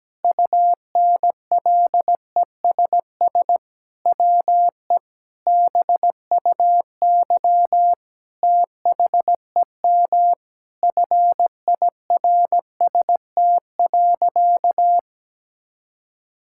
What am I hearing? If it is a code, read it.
Morse code, heard as a keyed tone, about 17 wpm, 700 Hz. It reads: UNLESS WE BUY THEM FIRST.
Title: Sentences from Top 1000 Words - Review Part 6 - 17wpm